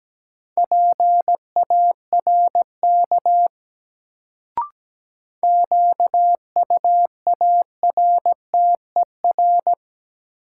Morse code, heard as a keyed tone, about 17 wpm, 700 Hz. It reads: PARK QUARTER
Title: Sentences from Top 1300 Words - New Word Drill - 17wpm